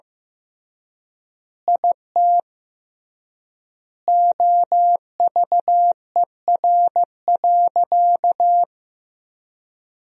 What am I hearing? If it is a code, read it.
Morse code, heard as a keyed tone, about 15 wpm, 700 Hz. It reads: IT OVER.
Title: 3x Word-Spacing - Sentences from Top 500 Words - 15wpm